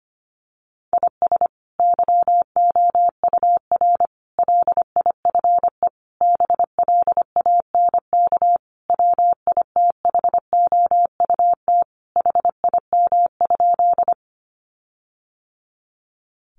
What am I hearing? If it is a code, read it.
Morse code, heard as a keyed tone, about 25 wpm, 700 Hz. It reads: IS YOUR LIFE BLANK WITHOUT HIM?
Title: Sentences from Top 1500 Words 25wpm